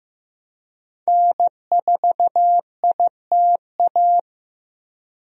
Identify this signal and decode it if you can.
Morse code, heard as a keyed tone, about 15 wpm, 700 Hz. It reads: N4ITA
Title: USA Call Signs - 5 Characters - 15wpm